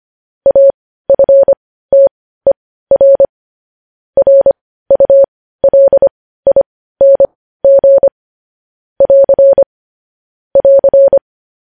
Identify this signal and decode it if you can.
Morse code, heard as a keyed tone, about 25 wpm, spaced as for 15 wpm, 550 Hz. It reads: AFTER RULING <AR> <AR>
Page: News Headlines in Morse Code at 15 WPM